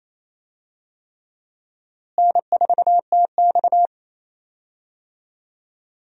Morse code, heard as a keyed tone, about 28 wpm, 700 Hz. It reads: N4TX
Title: USA Call Signs - 4 Characters - Repeated 3x - 28wpm